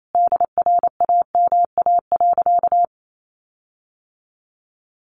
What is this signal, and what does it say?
Morse code, heard as a keyed tone, about 28 wpm, 700 Hz. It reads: DRAMA.
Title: Sentences from Top 1500 Words 28wpm